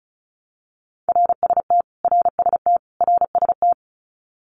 Morse code, heard as a keyed tone, about 34 wpm, 700 Hz. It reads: RST RST RST
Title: Abbreviations Repeated 3x - 35wpm